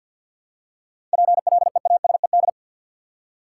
Morse code, heard as a keyed tone, about 52 wpm, 700 Hz. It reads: PLEASED
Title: Seven Letter Words - 50wpm